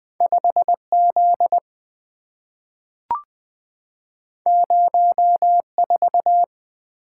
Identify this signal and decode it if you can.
Morse code, heard as a keyed tone, about 20 wpm, 690 Hz. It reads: 5Z 04
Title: Two Letter-Number Combinations - 20wpm